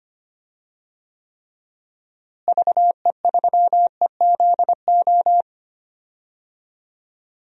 Morse code, heard as a keyed tone, about 25 wpm, 700 Hz. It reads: VE3EZO
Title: DXCC-100 Call Signs - 6 Characters - Repeated 3x - 25wpm